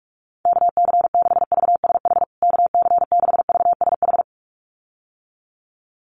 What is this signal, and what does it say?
Morse code, heard as a keyed tone, about 44 wpm, 700 Hz. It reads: KC6VSH KC6VSH
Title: USA Call Signs Repeated 2x 45wpm